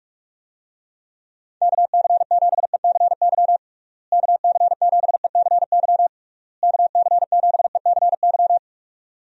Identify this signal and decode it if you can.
Morse code, heard as a keyed tone, about 41 wpm, 700 Hz. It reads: KC7ECY KC7ECY KC7ECY
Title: USA Call Signs - 6 Characters - Repeated 3x - 45wpm